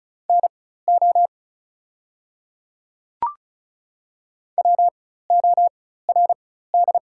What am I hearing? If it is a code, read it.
Morse code, heard as a keyed tone, about 36 wpm, spaced as for 16 wpm, 690 Hz. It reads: NO WORD
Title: 4x Character Spacing - Top 100 Words - 35wpm